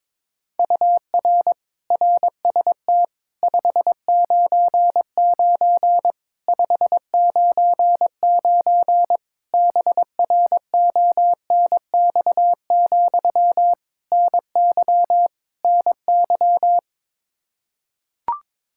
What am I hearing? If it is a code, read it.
Morse code, heard as a keyed tone, about 22 wpm, 700 Hz. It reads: UR RST 599 599 BRONX, NY NY
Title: Practice QSO - Exchange 1 - Beginning - 22wpm